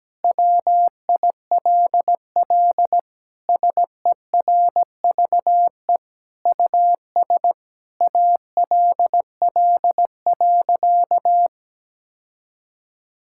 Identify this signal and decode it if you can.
Morse code, heard as a keyed tone, about 17 wpm, 700 Hz. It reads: WILL SERVE US ALL.
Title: Sentences from Top 1000 Words - Review Part 5 - 17wpm